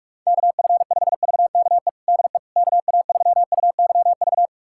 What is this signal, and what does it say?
Morse code, heard as a keyed tone, about 41 wpm, 700 Hz. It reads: KF5VKE DE KA3UYV
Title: USA Callsign DE Callsign - 6 Characters - 45wpm